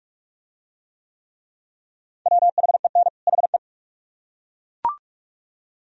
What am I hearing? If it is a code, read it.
Morse code, heard as a keyed tone, about 42 wpm, 700 Hz. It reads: WHEN HE